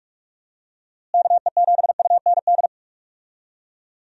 Morse code, heard as a keyed tone, about 46 wpm, 700 Hz. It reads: KE7UND
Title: USA Call Signs - 6 Characters - Repeated 3x - 45wpm